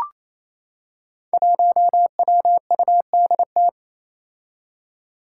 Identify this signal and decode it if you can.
Morse code, heard as a keyed tone, about 27 wpm, 700 Hz. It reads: 1WUDT
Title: Five Challenging Letter-Number Combinations 28wpm